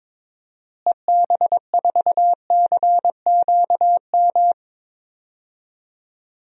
Morse code, heard as a keyed tone, about 22 wpm, 700 Hz. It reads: EB4CQM
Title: DXCC-100 Call Signs - 6 Characters - Repeated 3x - 22wpm